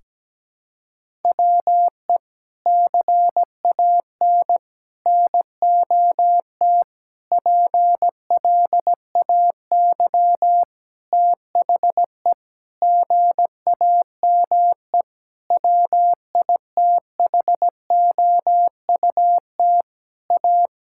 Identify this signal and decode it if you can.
Morse code, heard as a keyed tone, about 17 wpm, 700 Hz. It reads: WE CAN NOT PLAY THE GAME WITHOUT A
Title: Sentences from Top 1000 Words - Review Part 6 - 17wpm